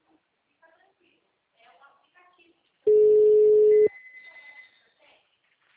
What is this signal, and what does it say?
Ringback tone (Europe, 425 Hz), with speech.